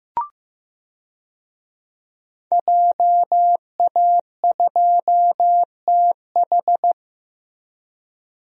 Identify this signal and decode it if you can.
Morse code, heard as a keyed tone, about 16 wpm, 700 Hz. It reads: JA2TH
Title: DXCC-100 Call Signs - 5 Characters - Repeated 3x - 15wpm